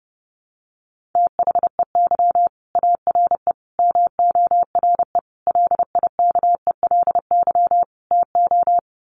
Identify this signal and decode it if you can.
Morse code, heard as a keyed tone, about 30 wpm, 700 Hz. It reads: THEY ARE MORE LIKELY TO
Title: Sets of 5 Words Encore2 30wpm